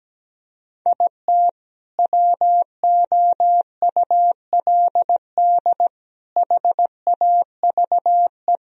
Morse code, heard as a keyed tone, about 17 wpm, 700 Hz. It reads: IT WOULD HAVE
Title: Sentences from Top 1000 Words - Review Part 1 - 17wpm